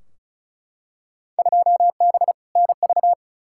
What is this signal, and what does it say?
Morse code, heard as a keyed tone, about 34 wpm, 700 Hz. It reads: 2B NV
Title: Field Day Exchanges Encore - 35wpm